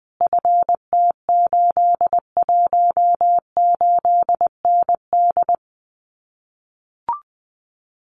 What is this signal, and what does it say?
Morse code, heard as a keyed tone, about 20 wpm, 700 Hz. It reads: FT818ND
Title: Common QSO Phrases Encore 20wpm